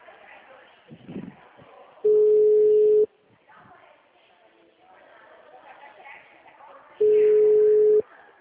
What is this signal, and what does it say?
Ringback tone (Europe, 425 Hz), with speech.